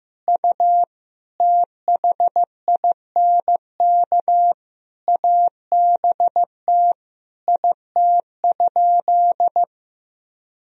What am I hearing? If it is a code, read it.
Morse code, heard as a keyed tone, about 15 wpm, 700 Hz. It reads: U THINK ABT IT?
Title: Sentences from Top 200 Words and Abbreviations 15wpm